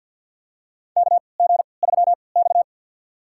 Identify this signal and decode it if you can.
Morse code, heard as a keyed tone, about 53 wpm, spaced as for 29 wpm, 700 Hz. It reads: KG3X
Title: USA Call Signs - 4 Characters - 3x Character Spacing - 50wpm